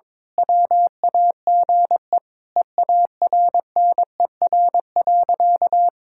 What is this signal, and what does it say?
Morse code, heard as a keyed tone, about 22 wpm, 700 Hz. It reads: WAGE EARNER.
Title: Pangram Challenge v2 - 22wpm